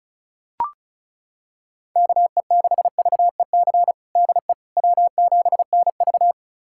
Morse code, heard as a keyed tone, about 34 wpm, 700 Hz. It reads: KE6VEC DE W7NV
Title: USA Callsign DE Callsign 35wpm